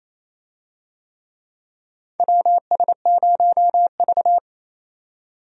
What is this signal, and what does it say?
Morse code, heard as a keyed tone, about 27 wpm, 700 Hz. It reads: WS0V
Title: USA Call Signs - 4 Characters - 28wpm